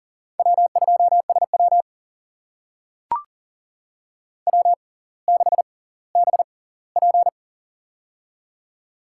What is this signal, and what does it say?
Morse code, heard as a keyed tone, about 40 wpm, spaced as for 20 wpm, 700 Hz. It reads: W2SW W6BP
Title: USA Call Signs - 4 Characters - 6x Character Spacing - 40wpm